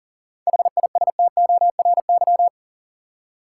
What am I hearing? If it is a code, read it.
Morse code, heard as a keyed tone, about 40 wpm, 700 Hz. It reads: HISTORY
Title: Seven Letter Words - 40wpm